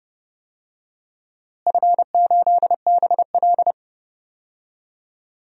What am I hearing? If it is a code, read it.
Morse code, heard as a keyed tone, about 30 wpm, 700 Hz. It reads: F8BL
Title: DXCC-100 Call Signs - 4 Characters - Repeated 2x - 30wpm